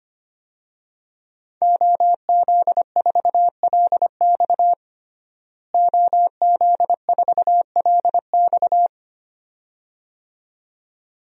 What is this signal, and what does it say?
Morse code, heard as a keyed tone, about 25 wpm, 700 Hz. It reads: OZ4LX OZ4LX
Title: DXCC-100 Call Signs - 5 Characters - Repeated 2x - 25wpm